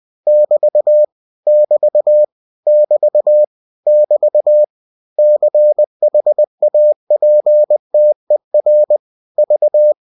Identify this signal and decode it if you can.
Morse code, heard as a keyed tone, about 20 wpm, 600 Hz. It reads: <BT> <BT> <BT> <BT> CHAPTER V
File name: cw-books / winnie-the-pooh / winnie-20wpm-page-0033